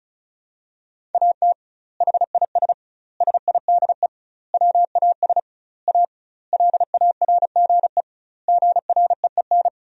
Morse code, heard as a keyed tone, about 34 wpm, 700 Hz. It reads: AT HIS SIDE WAS A LARGE GREEN
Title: 2x Word-Spacing - Sentences from Top 500 Words - 35wpm